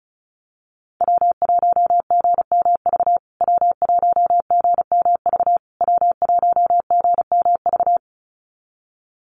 Morse code, heard as a keyed tone, about 35 wpm, 700 Hz. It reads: W1GMV W1GMV W1GMV
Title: USA Call Signs Repeated 3x 35wpm